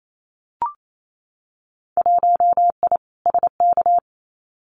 Morse code, heard as a keyed tone, about 28 wpm, 700 Hz. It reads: 1I SK
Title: Winter Field Day Exchanges Encore 28wpm